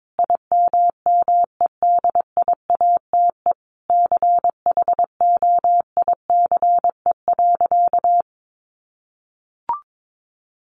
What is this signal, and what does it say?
Morse code, heard as a keyed tone, about 22 wpm, 700 Hz. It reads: IMMEDIATE CHOICE.
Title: Sentences from Top 1200 Words 22wpm